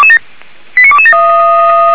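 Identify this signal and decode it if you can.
Five-tone selective call (ZVEI), heard as quick runs of five tones, tones of about 70 ms. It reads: 89227
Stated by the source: ZVEI five-tone selective call received by SDR.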